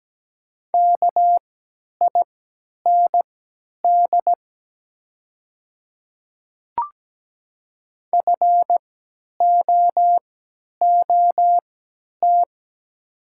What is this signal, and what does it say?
Morse code, heard as a keyed tone, about 17 wpm, spaced as for 10 wpm, 700 Hz. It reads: KIND FOOT
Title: 3x Character Spacing - Top 500 Words - 17wpm